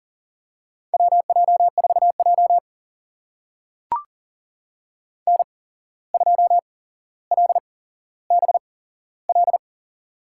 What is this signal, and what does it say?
Morse code, heard as a keyed tone, about 40 wpm, spaced as for 16 wpm, 700 Hz. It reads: WJ4J N2LBL
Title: USA Call Signs - 8x Character Spacing - 40wpm